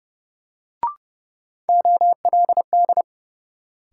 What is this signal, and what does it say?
Morse code, heard as a keyed tone, about 30 wpm, 700 Hz. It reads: OLD
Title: Sentences from Top 200 Words - New Word Drill - 30wpm